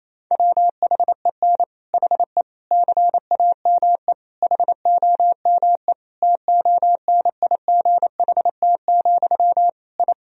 Morse code, heard as a keyed tone, about 28 wpm, 700 Hz. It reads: WHEN HE CAME HOME TONIGHT, I
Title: Sentences from Top 1000 Words - Review Part 3 - 28wpm